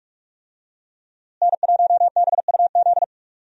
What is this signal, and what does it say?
Morse code, heard as a keyed tone, about 46 wpm, 700 Hz. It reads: N1BUZ
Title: USA Call Signs - 5 Characters - Repeated 2x - 45wpm